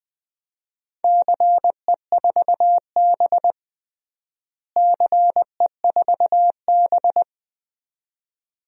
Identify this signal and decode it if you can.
Morse code, heard as a keyed tone, about 20 wpm, 700 Hz. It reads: CE4B CE4B
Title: DXCC-100 Call Signs - 4 Characters - Repeated 2x - 20wpm